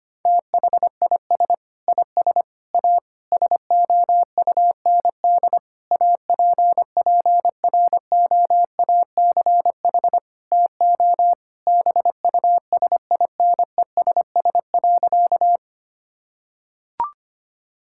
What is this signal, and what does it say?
Morse code, heard as a keyed tone, about 25 wpm, 700 Hz. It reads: THIS IS A SOUND APPROACH TO BUSINESS.
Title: Sentences from Top 1000 Words - Review Part 6 - 25wpm